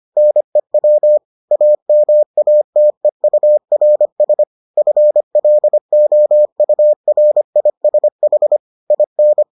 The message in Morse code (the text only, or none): NEW AMATEURS FLOURISH IN